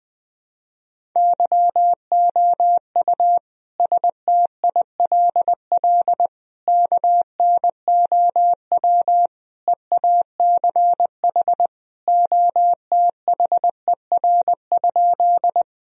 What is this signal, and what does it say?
Morse code, heard as a keyed tone, about 20 wpm, 700 Hz. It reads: YOU STILL KNOW EACH OTHER?
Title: Sentences from Top 1000 Words - Review Part 1 - 20wpm